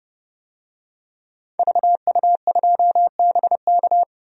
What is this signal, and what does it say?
Morse code, heard as a keyed tone, about 32 wpm, 700 Hz. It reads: VU2BK
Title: DXCC-100 Call Signs - 5 Characters - Repeated 3x - 30wpm